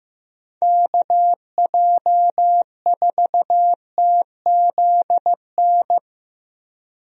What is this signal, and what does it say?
Morse code, heard as a keyed tone, about 15 wpm, 700 Hz. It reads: KJ4TZN
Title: USA Call Signs - 6 Characters - Repeated 2x - 15wpm